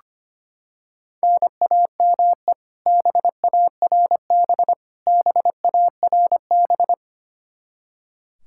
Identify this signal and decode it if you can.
Morse code, heard as a keyed tone, about 25 wpm, 700 Hz. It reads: NAME BARB BARB
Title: Practice QSO - Exchange 1 - Beginning - 25wpm